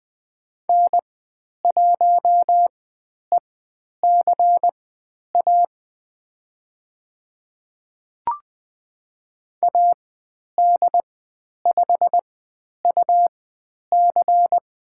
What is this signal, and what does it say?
Morse code, heard as a keyed tone, about 20 wpm, spaced as for 10 wpm, 700 Hz. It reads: N1ECA AD5UC
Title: 102 Lesson 17g - Letter u - Call Signs 20wpm